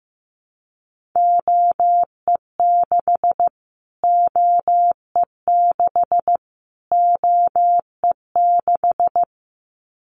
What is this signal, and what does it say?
Morse code, heard as a keyed tone, about 15 wpm, 700 Hz. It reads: OE6 OE6 OE6
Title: Callsign Prefixes for Top 25 DXCC - Repeated 3x - 15wpm